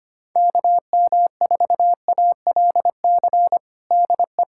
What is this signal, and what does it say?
Morse code, heard as a keyed tone, about 25 wpm, 700 Hz. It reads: KM4ALC DE
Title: USA Callsign DE Callsign 25wpm